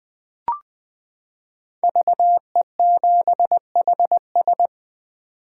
Morse code, heard as a keyed tone, about 20 wpm, 700 Hz. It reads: VE7HS
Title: K1USN SST QSO Practice - 20wpm